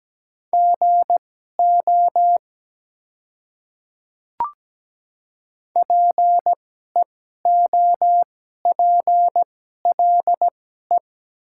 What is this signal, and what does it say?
Morse code, heard as a keyed tone, about 17 wpm, spaced as for 12 wpm, 700 Hz. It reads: GO PEOPLE
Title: 2x Character Spacing - Top 100 Words - 17wpm